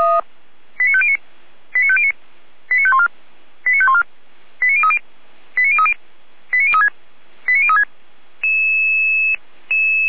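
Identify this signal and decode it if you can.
Five-tone selective call (ZVEI), heard as quick runs of five tones, tones of about 70 ms. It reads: 89509 89509 89514 89514 89030 89030 89036 89036
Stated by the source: ZVEI five-tone call 89001 received by SDR, archived by Feuerwehr Orlamünde.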